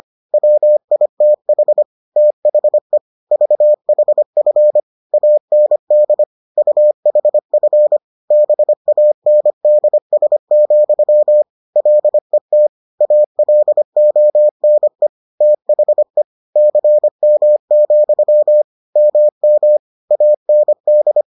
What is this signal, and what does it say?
Morse code, heard as a keyed tone, about 25 wpm, 600 Hz. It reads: WITH THE VHF AND UHF BANDS, LET ALONE THE CM, MM AND